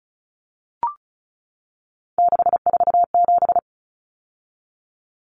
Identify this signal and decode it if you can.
Morse code, heard as a keyed tone, about 34 wpm, 700 Hz. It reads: 647
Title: Numbers - 3-Digit 35wpm